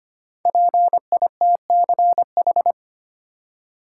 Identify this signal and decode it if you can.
Morse code, heard as a keyed tone, about 25 wpm, 700 Hz. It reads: PITCH
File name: 5 Letter Words - First 3 Letters Given - 25wpm